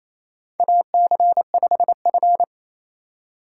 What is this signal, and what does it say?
Morse code, heard as a keyed tone, about 28 wpm, 700 Hz. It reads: AC5F
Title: USA Call Signs - 2x Character Spacing - 28wpm